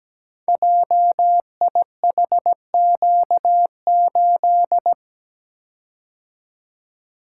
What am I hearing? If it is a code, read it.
Morse code, heard as a keyed tone, about 17 wpm, 700 Hz. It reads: JIHQ8
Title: Code Groups - 5 Letter:Numbers - 17wpm